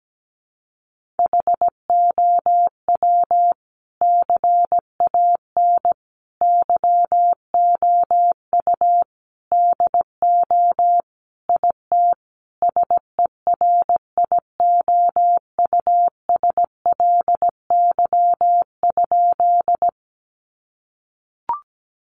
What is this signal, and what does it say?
Morse code, heard as a keyed tone, about 17 wpm, 700 Hz. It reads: HOW CAN YOU DO IT SERIOUSLY?
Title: Sentences from Top 1200 Words 17wpm